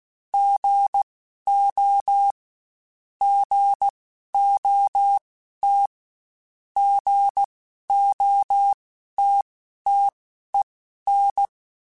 Morse code, heard as a keyed tone, about 16 wpm, spaced as for 12 wpm, 800 Hz. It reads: GO GOT GOTTEN